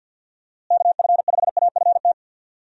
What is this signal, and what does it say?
Morse code, heard as a keyed tone, about 50 wpm, 700 Hz. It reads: KF5AFT
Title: USA Call Signs - 6 Characters - Repeated 3x - 50wpm